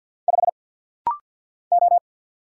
Morse code, heard as a keyed tone, about 48 wpm, 700 Hz. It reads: 5 O
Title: Single Letter-Number - Rapid-Fire 50wpm